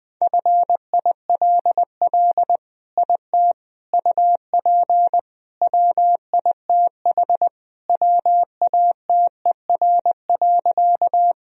Morse code, heard as a keyed tone, about 20 wpm, 700 Hz. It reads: FILL IT UP WITH WATER.
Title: Sentences from Top 1000 Words - Review Part 3 - 20wpm